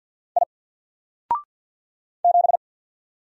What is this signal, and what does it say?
Morse code, heard as a keyed tone, about 48 wpm, 700 Hz. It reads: I 7
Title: Single Character (Letter:Number:Punctuation:Prosign) 50wpm